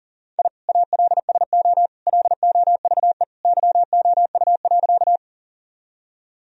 Morse code, heard as a keyed tone, about 40 wpm, 700 Hz. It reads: I ALSO LOVE YOU.
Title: Sentences from Top 1000 Words - Review Part 1 - 40wpm